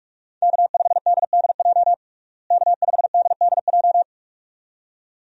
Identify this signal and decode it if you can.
Morse code, heard as a keyed tone, about 46 wpm, 700 Hz. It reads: K5DDJ K5DDJ
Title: USA Call Signs - 5 Characters - Repeated 2x - 45wpm